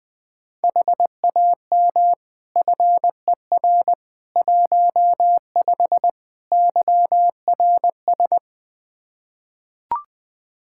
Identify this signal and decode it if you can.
Morse code, heard as a keyed tone, about 20 wpm, 700 Hz. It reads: HAM FER 15 YRS
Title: K1USN SST QSO Practice 20wpm